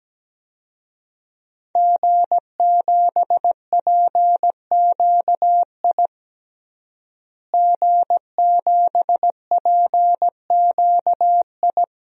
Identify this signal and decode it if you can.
Morse code, heard as a keyed tone, about 17 wpm, 700 Hz. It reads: G7PQI G7PQI
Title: DXCC-100 Call Signs - 5 Characters - Repeated 2x - 17wpm